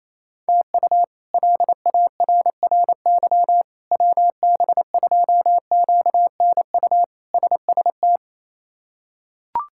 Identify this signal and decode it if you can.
Morse code, heard as a keyed tone, about 28 wpm, 700 Hz. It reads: TU LARRY WB2QNU SST
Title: K1USN SST QSO Practice - 28wpm